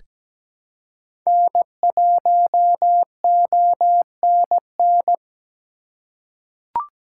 Morse code, heard as a keyed tone, about 17 wpm, 700 Hz. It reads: N1ONN
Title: USA Call Signs - 5 Characters - 3x Character Spacing - 17wpm